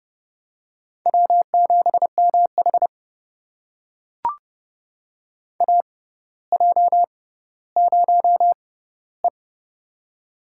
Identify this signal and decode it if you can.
Morse code, heard as a keyed tone, about 32 wpm, spaced as for 16 wpm, 700 Hz. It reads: W7MH AJ0E
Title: USA Call Signs - 4 Characters - 6x Character Spacing - 30wpm